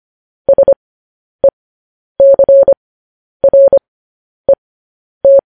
Morse code, heard as a keyed tone, about 25 wpm, spaced as for 10 wpm, 550 Hz. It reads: SECRET